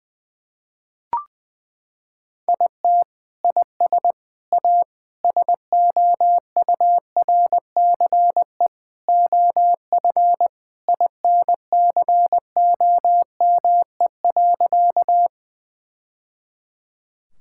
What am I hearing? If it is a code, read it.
Morse code, heard as a keyed tone, about 20 wpm, 700 Hz. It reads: IT IS A SOURCE OF INCOME.
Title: Sentences from Top 1000 Words - Review Part 5 - 20wpm